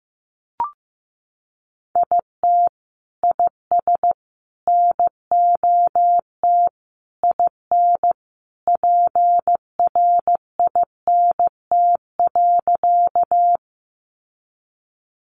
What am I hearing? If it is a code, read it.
Morse code, heard as a keyed tone, about 15 wpm, 700 Hz. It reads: IT IS NOT IN PRINT.
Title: Sentences from Top 1400 Words 15wpm